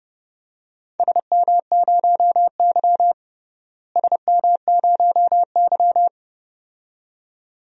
Morse code, heard as a keyed tone, about 30 wpm, 700 Hz. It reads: SM0Y SM0Y
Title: DXCC-100 Call Signs - 4 Characters - Repeated 2x - 30wpm